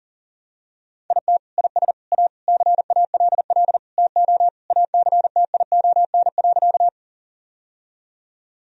Morse code, heard as a keyed tone, about 40 wpm, 700 Hz. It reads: IT IS A CALL TO ACTION.
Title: Sentences from Top 1000 Words - Review Part 4 - 40wpm